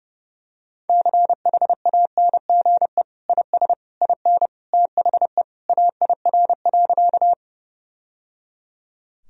Morse code, heard as a keyed tone, about 30 wpm, 700 Hz. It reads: CHANGE IS IN THE AIR.
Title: Sentences from Top 1000 Words - Review Part 1 - 30wpm